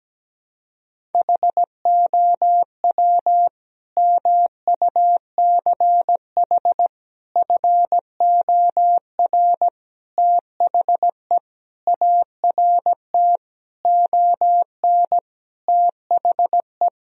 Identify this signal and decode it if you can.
Morse code, heard as a keyed tone, about 17 wpm, 700 Hz. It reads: HOW MUCH FOR THE ART ON THE
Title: Sentences from Top 1000 Words - Review Part 3 - 17wpm